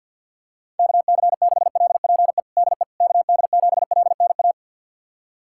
Morse code, heard as a keyed tone, about 53 wpm, 690 Hz. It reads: KC6LPE DE KD7LNA
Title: USA Callsign DE Callsign - 6 Characters - 50wpm